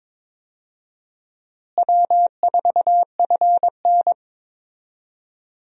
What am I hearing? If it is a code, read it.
Morse code, heard as a keyed tone, about 22 wpm, 700 Hz. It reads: W4FN
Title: USA Call Signs - 4 Characters - Repeated 2x - 22wpm